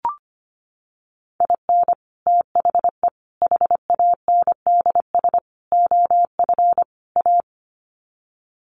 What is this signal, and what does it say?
Morse code, heard as a keyed tone, about 25 wpm, 700 Hz. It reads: IN THE HANDS OF A
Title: Sets of 5 Words Encore 25wpm